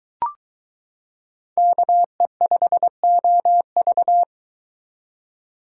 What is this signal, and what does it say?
Morse code, heard as a keyed tone, about 23 wpm, 700 Hz. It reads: KE5OV